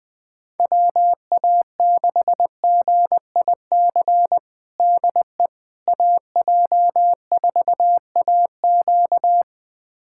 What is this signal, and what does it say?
Morse code, heard as a keyed tone, about 20 wpm, 700 Hz. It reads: WA6GIC DE AJ4AQ
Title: USA Callsign DE Callsign 20wpm